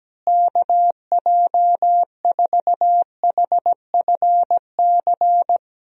Morse code, heard as a keyed tone, about 17 wpm, 700 Hz. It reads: KJ4HFC
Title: USA Call Signs - 6 Characters - Repeated 3x - 17wpm